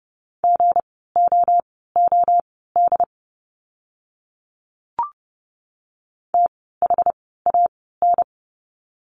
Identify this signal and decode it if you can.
Morse code, heard as a keyed tone, about 30 wpm, spaced as for 17 wpm, 700 Hz. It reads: GOOD THAN
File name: course / 30:10wpm / 230 Lesson 37f - Letter x - Words 30wpm